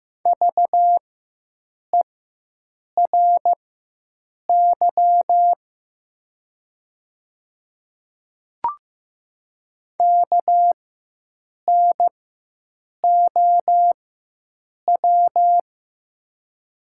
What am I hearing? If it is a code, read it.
Morse code, heard as a keyed tone, about 15 wpm, spaced as for 7 wpm, 700 Hz. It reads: VERY KNOW
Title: 4x Character Spacing - Top 300 Words - 15wpm